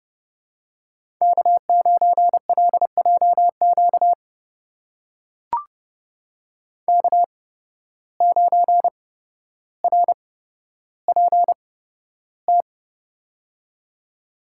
Morse code, heard as a keyed tone, about 30 wpm, spaced as for 13 wpm, 700 Hz. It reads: K9LJQ K9RPT
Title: USA Call Signs - 8x Character Spacing - 30wpm